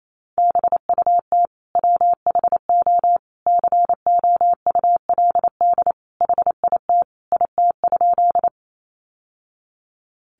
Morse code, heard as a keyed tone, about 28 wpm, 700 Hz. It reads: BUT WHO COULD HIT IT?
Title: Sentences from Top 1200 Words 28wpm